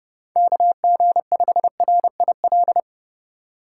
Morse code, heard as a keyed tone, about 30 wpm, 700 Hz. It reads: KG5RIL
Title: USA Call Signs - 2x Character Spacing - 30wpm